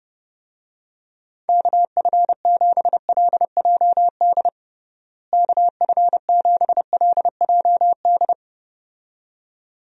Morse code, heard as a keyed tone, about 31 wpm, 700 Hz. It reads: KF7LJD KF7LJD
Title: USA Call Signs - 6 Characters - Repeated 2x - 30wpm